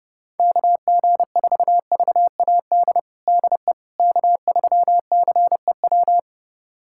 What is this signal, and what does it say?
Morse code, heard as a keyed tone, about 30 wpm, 700 Hz. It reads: KG4VAD DE K3CEW
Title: USA Callsign DE Callsign 30wpm